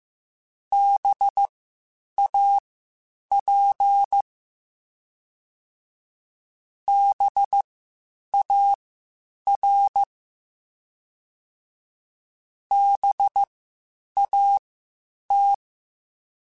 Morse code, heard as a keyed tone, about 15 wpm, spaced as for 8 wpm, 800 Hz. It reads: BAP BAR BAT